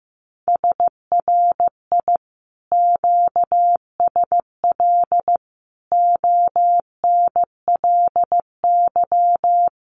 Morse code, heard as a keyed tone, about 16 wpm, 700 Hz. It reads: SRI QSL ONLY
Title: Common QSO Phrases Encore 15wpm